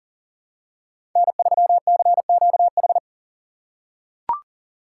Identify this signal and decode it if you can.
Morse code, heard as a keyed tone, about 40 wpm, 700 Hz. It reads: N3CQH
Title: USA Call Signs - 3x Character Spacing - 40wpm